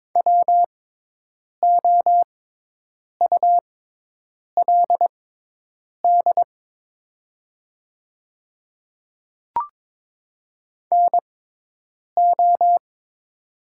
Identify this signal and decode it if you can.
Morse code, heard as a keyed tone, about 22 wpm, spaced as for 8 wpm, 700 Hz. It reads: WOULD NO
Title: 6x Character Spacing - Top 100 Words - 22wpm